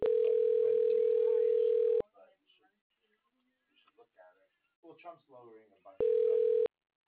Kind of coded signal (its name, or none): telephone call progress tone